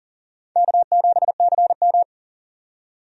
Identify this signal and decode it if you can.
Morse code, heard as a keyed tone, about 40 wpm, 700 Hz. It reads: K7CM
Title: USA Call Signs - 4 Characters - 40wpm